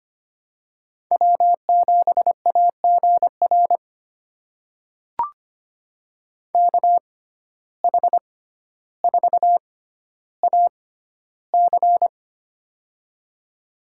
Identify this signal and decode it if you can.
Morse code, heard as a keyed tone, about 25 wpm, spaced as for 13 wpm, 700 Hz. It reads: W7AGR KH4AC
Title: USA Call Signs - 5 Characters - 6x Character Spacing - 25wpm